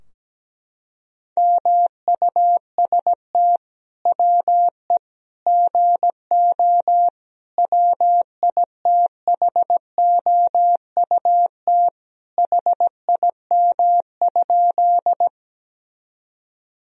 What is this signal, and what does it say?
Morse code, heard as a keyed tone, about 17 wpm, 700 Hz. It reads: MUST WE GO WITHOUT HIM?
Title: Sentences from Top 1000 Words - Review Part 3 - 17wpm